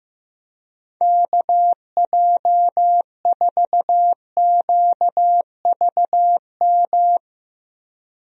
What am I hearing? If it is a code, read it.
Morse code, heard as a keyed tone, about 15 wpm, 700 Hz. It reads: KJ4QVM
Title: USA Call Signs - 6 Characters - Repeated 2x - 15wpm